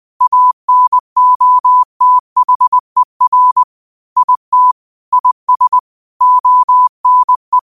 Morse code, heard as a keyed tone, about 20 wpm, 1000 Hz. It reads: ANOTHER IT IS ONE
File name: qotdfiles / 20wpm